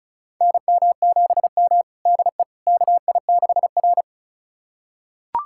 Morse code, heard as a keyed tone, about 36 wpm, 700 Hz. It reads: NM7M DE KI6R
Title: USA Callsign DE Callsign - 4 Characters - 35wpm